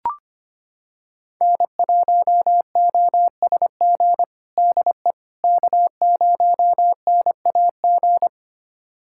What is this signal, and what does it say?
Morse code, heard as a keyed tone, about 25 wpm, 700 Hz. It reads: N1OSG DE K0NAG
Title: USA Callsign DE Callsign - 5 Characters - 25wpm